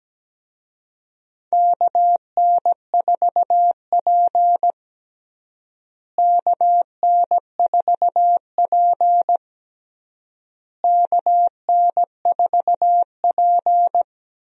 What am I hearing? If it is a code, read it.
Morse code, heard as a keyed tone, about 17 wpm, 700 Hz. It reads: KN4P KN4P KN4P
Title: USA Call Signs - 4 Characters - Repeated 3x - 17wpm